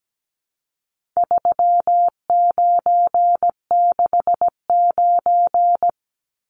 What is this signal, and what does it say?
Morse code, heard as a keyed tone, about 17 wpm, 700 Hz. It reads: 3969
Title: Numbers - 4-Digit 17wpm